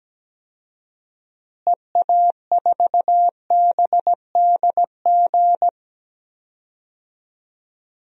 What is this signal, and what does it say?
Morse code, heard as a keyed tone, about 17 wpm, 700 Hz. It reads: EA4BDG
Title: DXCC-100 Call Signs - 6 Characters - Repeated 3x - 17wpm